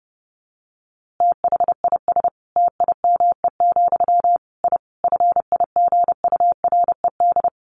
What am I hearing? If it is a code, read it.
Morse code, heard as a keyed tone, about 30 wpm, 700 Hz. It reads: THIS TIME, I FIGURED
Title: A Sound Of Thunder 30wpm